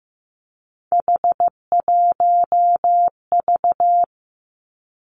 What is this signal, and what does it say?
Morse code, heard as a keyed tone, about 15 wpm, 700 Hz. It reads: H1V
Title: Three Challenging Letter-Number Combinations 15wpm